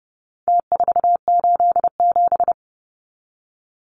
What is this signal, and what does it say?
Morse code, heard as a keyed tone, about 30 wpm, 700 Hz. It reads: T487
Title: Numbers - Contest Style - 4-Digit 30wpm